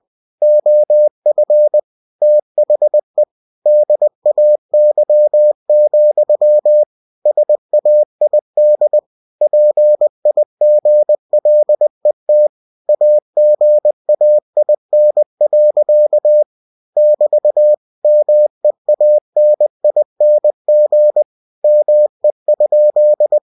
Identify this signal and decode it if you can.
Morse code, heard as a keyed tone, about 20 wpm, 600 Hz. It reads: OF THE DAY, SAID PIGLET AGAIN. <BT> MEANING ME?
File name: cw-books / winnie-the-pooh / winnie-20wpm-page-0054